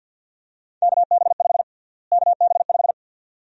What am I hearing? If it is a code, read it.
Morse code, heard as a keyed tone, about 52 wpm, 700 Hz. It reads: KB5 KB5
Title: USA Call Sign Prefixes - Repeated 2x - 50wpm